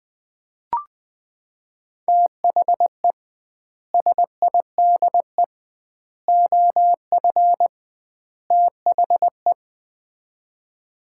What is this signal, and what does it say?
Morse code, heard as a keyed tone, about 20 wpm, 700 Hz. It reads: THE SIDE OF THE
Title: 2x Word-Spacing - Sets of 4 Words - 20wpm